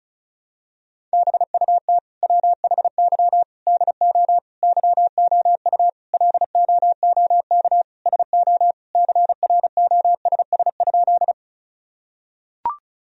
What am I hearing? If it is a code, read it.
Morse code, heard as a keyed tone, about 34 wpm, 700 Hz. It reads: BUT WHY DO YOU LOOK SO CROSS?
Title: Sentences from Top 1000 Words - Review Part 1 - 35wpm